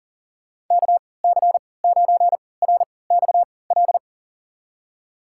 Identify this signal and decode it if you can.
Morse code, heard as a keyed tone, about 37 wpm, spaced as for 22 wpm, 700 Hz. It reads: KC9RXL
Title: USA Call Signs - 6 Characters - 3x Character Spacing - 40wpm